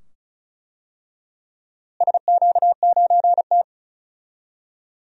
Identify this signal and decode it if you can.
Morse code, heard as a keyed tone, about 36 wpm, 700 Hz. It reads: SQ9T
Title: DXCC-100 Call Signs - 4 Characters - Repeated 2x - 35wpm